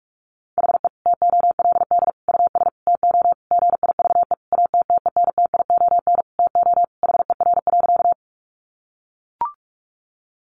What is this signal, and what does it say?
Morse code, heard as a keyed tone, about 44 wpm, 700 Hz. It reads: HE TOLD US TO GIVE ATTENTION TO HER.
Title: Sentences from Top 1000 Words 45wpm